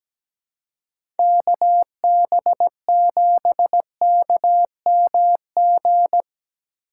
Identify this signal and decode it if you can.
Morse code, heard as a keyed tone, about 17 wpm, 700 Hz. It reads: KB7KMG
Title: USA Call Signs - 6 Characters - 3x Character Spacing - 17wpm